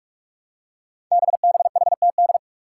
Morse code, heard as a keyed tone, about 44 wpm, 700 Hz. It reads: BBHTD
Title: Five Challenging Letter-Number Combinations 45wpm